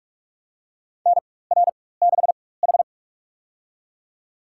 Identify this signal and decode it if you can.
Morse code, heard as a keyed tone, about 44 wpm, spaced as for 21 wpm, 690 Hz. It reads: NR6H
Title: USA Call Signs - 4x Character Spacing - 45wpm